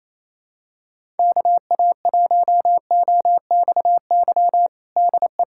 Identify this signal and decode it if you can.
Morse code, heard as a keyed tone, about 28 wpm, 700 Hz. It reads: KA1OXY DE
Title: USA Callsign DE Callsign - 6 Characters - 28wpm